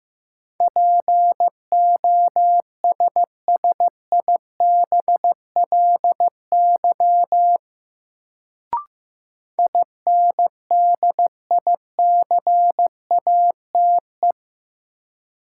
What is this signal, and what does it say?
Morse code, heard as a keyed tone, about 15 wpm, 700 Hz. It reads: POSSIBLY INDICATE
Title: Sentences from Top 1300 Words - New Word Drill Alt - 15wpm